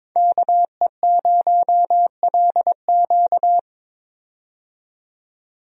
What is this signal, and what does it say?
Morse code, heard as a keyed tone, about 22 wpm, 700 Hz. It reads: KE0LQ
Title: USA Call Signs - 5 Characters - 22wpm